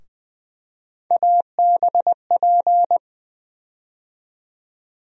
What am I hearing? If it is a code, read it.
Morse code, heard as a keyed tone, about 21 wpm, 700 Hz. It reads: ABP
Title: USA Call Sign Postfixes - 3 Letters 20wpm